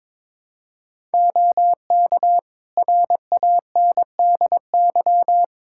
Morse code, heard as a keyed tone, about 22 wpm, 700 Hz. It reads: OK RANDY
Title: Practice QSO - Exchange 3 - More Info - 22wpm